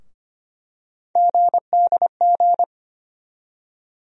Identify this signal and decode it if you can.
Morse code, heard as a keyed tone, about 25 wpm, 700 Hz. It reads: GDG
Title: USA Call Sign Postfixes - 3 Letters 25wpm